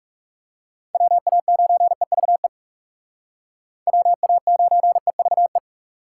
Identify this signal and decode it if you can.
Morse code, heard as a keyed tone, about 37 wpm, 700 Hz. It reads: WA9EVE WA9EVE
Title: Speed Racing - USA 2x3 Call Signs 30wpm